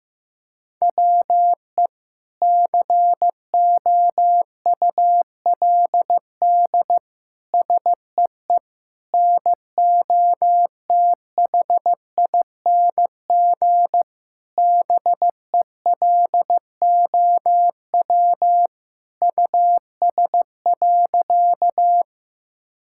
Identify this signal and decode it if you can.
Morse code, heard as a keyed tone, about 15 wpm, 700 Hz. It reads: WE COULD SEE NOTHING BELOW US.
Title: Sentences from Top 1000 Words - Review Part 5 - 15wpm